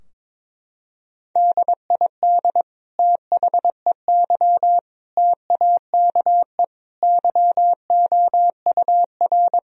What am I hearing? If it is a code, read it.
Morse code, heard as a keyed tone, about 23 wpm, 700 Hz. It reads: DID THEY TAKE YOUR
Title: Sentences from Top 1000 Words - Review Part 3 - 22wpm